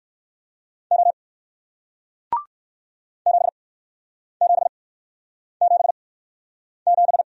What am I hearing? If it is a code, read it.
Morse code, heard as a keyed tone, about 54 wpm, spaced as for 9 wpm, 700 Hz. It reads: O 7777
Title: Speed-Racing Single Letter-Number 45wpm